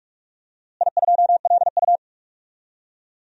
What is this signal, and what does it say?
Morse code, heard as a keyed tone, about 43 wpm, 700 Hz. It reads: I2LU
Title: DXCC-100 Call Signs - 4 Characters - Repeated 2x - 45wpm